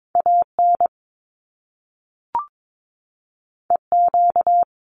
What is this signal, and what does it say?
Morse code, heard as a keyed tone, about 22 wpm, 700 Hz. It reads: AN EQ
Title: Top Two Letter Combinations in Words 22wpm